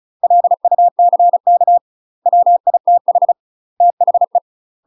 Morse code, heard as a keyed tone, about 34 wpm, 700 Hz. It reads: LUCK WITH THE
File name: UK MASTER QSOs TRAINING / 35WPM / UK MASTER QSOs 0028